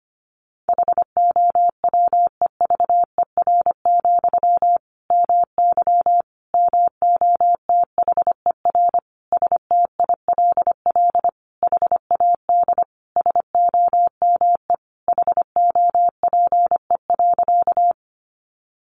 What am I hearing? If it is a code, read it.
Morse code, heard as a keyed tone, about 25 wpm, 700 Hz. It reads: HOWEVER, MY MOTHER STILL HAD SOME HOPE.
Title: Sentences from Top 1000 Words 25wpm